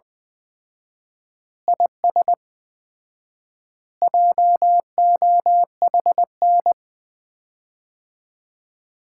Morse code, heard as a keyed tone, about 21 wpm, 700 Hz. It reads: IS JOHN
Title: Common QSO Phrases - 4x Word-Spacing - 20wpm